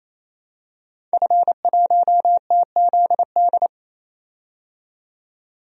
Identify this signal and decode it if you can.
Morse code, heard as a keyed tone, about 27 wpm, 700 Hz. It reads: F1TZD
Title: DXCC-100 Call Signs - 5 Characters - 28wpm